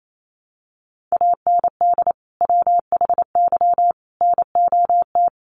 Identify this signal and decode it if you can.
Morse code, heard as a keyed tone, about 28 wpm, 700 Hz. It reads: AND WHY NOT
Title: Sentences from Top 1300 Words 28wpm